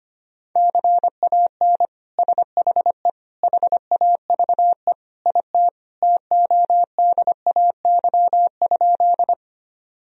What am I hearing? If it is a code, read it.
Morse code, heard as a keyed tone, about 25 wpm, 700 Hz. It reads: CAN SHE HAVE IT TODAY?
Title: Sentences from Top 1000 Words - Review Part 4 - 25wpm